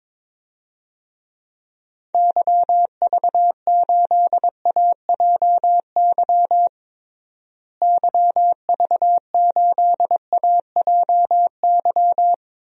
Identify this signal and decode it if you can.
Morse code, heard as a keyed tone, about 22 wpm, 700 Hz. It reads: YV8AJY YV8AJY
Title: DXCC-100 Call Signs - 6 Characters - Repeated 3x - 22wpm